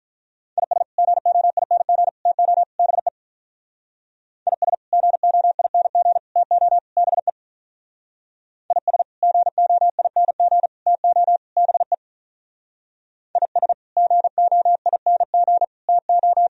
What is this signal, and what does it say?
Morse code, heard as a keyed tone, about 46 wpm, 700 Hz. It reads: IS GOING TO BE IS GOING TO BE IS GOING TO BE IS GOING TO